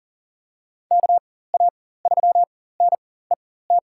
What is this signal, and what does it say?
Morse code, heard as a keyed tone, about 40 wpm, spaced as for 19 wpm, 700 Hz. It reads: KA3NET
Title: USA Call Signs - 6 Characters - 4x Character Spacing - 40wpm